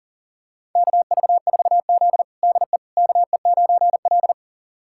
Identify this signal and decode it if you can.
Morse code, heard as a keyed tone, about 37 wpm, 700 Hz. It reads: KV4Z DE KE9L
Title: USA Callsign DE Callsign - 4 Characters - 40wpm